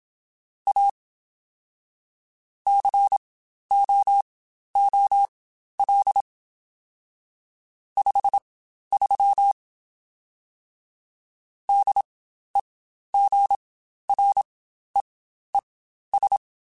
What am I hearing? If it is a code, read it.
Morse code, heard as a keyed tone, about 27 wpm, spaced as for 12 wpm, 800 Hz. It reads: A COOL 53 DEGREES
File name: Spaced Characters / Spaced Characters-mp3s / Weather 5 1 sp @25